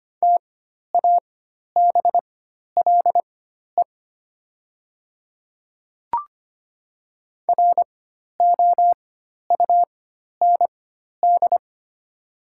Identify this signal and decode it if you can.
Morse code, heard as a keyed tone, about 25 wpm, spaced as for 12 wpm, 700 Hz. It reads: TABLE ROUND
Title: 4x Character Spacing - Top 500 Words - 25wpm